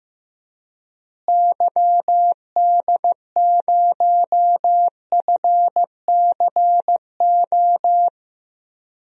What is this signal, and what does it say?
Morse code, heard as a keyed tone, about 15 wpm, 700 Hz. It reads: YD0FCO
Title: DXCC-100 Call Signs - 6 Characters - Repeated 2x - 15wpm